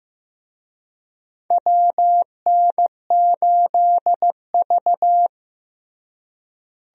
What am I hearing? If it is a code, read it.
Morse code, heard as a keyed tone, about 15 wpm, 700 Hz. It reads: WN8V
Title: USA Call Signs - 4 Characters - Repeated 2x - 15wpm